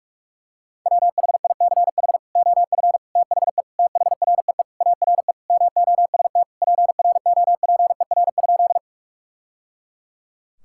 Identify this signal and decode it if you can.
Morse code, heard as a keyed tone, about 47 wpm, 700 Hz. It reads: WHICH OF THE THREE ARE MOST PROPER?
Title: Sentences from Top 1000 Words - Review Part 5 - 45wpm